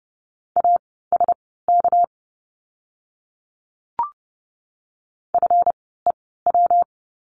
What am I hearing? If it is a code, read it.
Morse code, heard as a keyed tone, about 30 wpm, spaced as for 17 wpm, 700 Hz. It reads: ASK FEW
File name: course / 30:10wpm / 230 Lesson 37f - Letter x - Words 30wpm